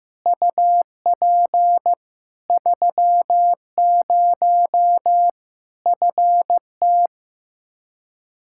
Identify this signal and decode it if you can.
Morse code, heard as a keyed tone, about 15 wpm, 700 Hz. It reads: UP 30 FT
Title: Practice QSO - Exchange 2 - Info - 15wpm